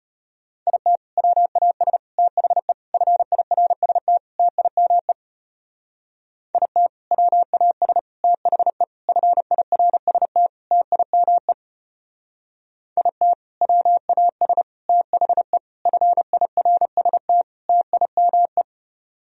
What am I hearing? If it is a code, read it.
Morse code, heard as a keyed tone, about 31 wpm, 700 Hz. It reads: IT WAS THE FIRST TIME IT WAS THE FIRST TIME IT WAS THE FIRST TIME
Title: Speed-Racing - Sets of 5 Words 25wpm